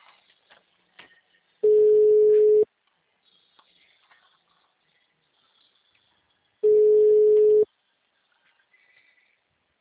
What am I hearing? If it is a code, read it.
Ringback tone (Europe, 425 Hz).